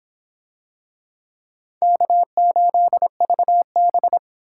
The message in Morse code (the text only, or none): K8VB